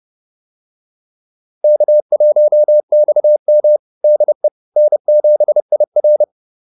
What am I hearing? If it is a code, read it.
Morse code, heard as a keyed tone, about 30 wpm, 600 Hz. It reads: K1XM DE N7IR